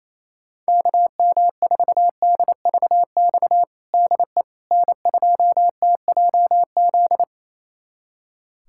Morse code, heard as a keyed tone, about 28 wpm, 700 Hz. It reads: KM4DVX DE N2TJZ
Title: USA Callsign DE Callsign 28wpm